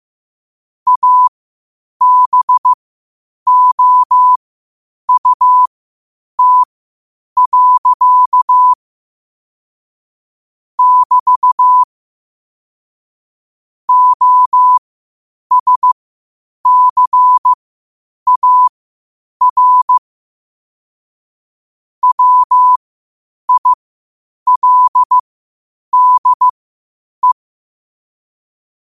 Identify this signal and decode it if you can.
Morse code, heard as a keyed tone, about 15 wpm, spaced as for 8 wpm, 1000 Hz. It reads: ABOUT. <BT> OSCAR WILDE
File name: qotdfiles / 8wpm